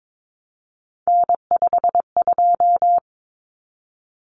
Morse code, heard as a keyed tone, about 22 wpm, 700 Hz. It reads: N52
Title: Common Cut Numbers - 3-Digit 22wpm